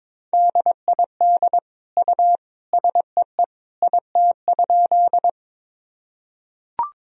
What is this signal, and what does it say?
Morse code, heard as a keyed tone, about 21 wpm, 700 Hz. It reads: DID U SEE IT?
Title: Sentences from Top 100 Words and Abbreviations 22wpm